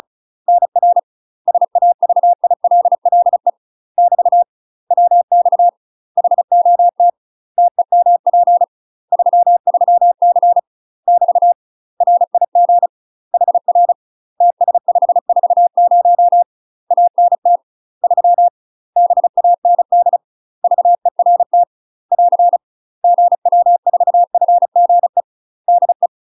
Morse code, heard as a keyed tone, about 36 wpm, 700 Hz. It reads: NR SAVILLE <BT> WX HOT TEMP 33C <BT> RIG HR TS540 ANT 3 BAND VERT <AR> GW4FGE DE
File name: UK MASTER QSOs TRAINING / 35WPM / UK MASTER QSOs 0036